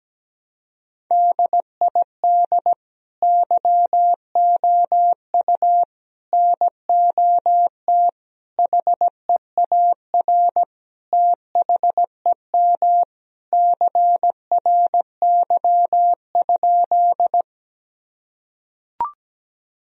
Morse code, heard as a keyed tone, about 17 wpm, 700 Hz. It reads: DID YOU NOT HEAR THEM CRY?
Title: Sentences from Top 1000 Words - Review Part 2 - 17wpm